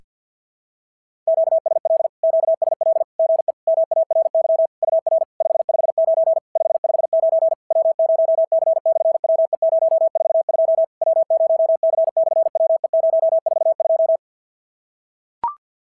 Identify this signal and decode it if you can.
Morse code, heard as a keyed tone, about 46 wpm, 660 Hz. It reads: QSL QSL GE MARY UR 559 559 W0C/WE042 W0C/WE042
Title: SOTA QSO Practice 50wpm